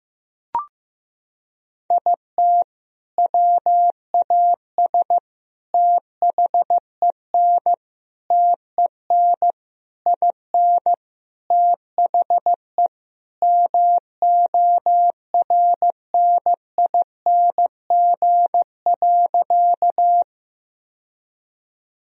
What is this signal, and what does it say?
Morse code, heard as a keyed tone, about 15 wpm, 700 Hz. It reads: IT WAS THEN TEN IN THE MORNING.
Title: Sentences from Top 1000 Words - Review Part 2 - 15wpm